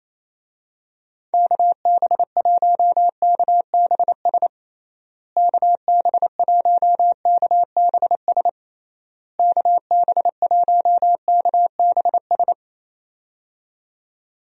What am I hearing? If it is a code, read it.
Morse code, heard as a keyed tone, about 28 wpm, 700 Hz. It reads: KB1KBS KB1KBS KB1KBS
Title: USA Call Signs - 6 Characters - Repeated 3x - 28wpm